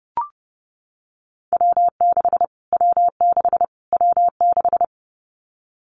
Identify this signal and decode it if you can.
Morse code, heard as a keyed tone, about 30 wpm, 700 Hz. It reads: W6 W6 W6
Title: Callsign Prefixes for Top 25 DXCC - Repeated 3x - 30wpm